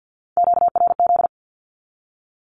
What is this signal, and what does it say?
Morse code, heard as a keyed tone, about 48 wpm, 700 Hz. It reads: QRZ
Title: Q-Codes 50wpm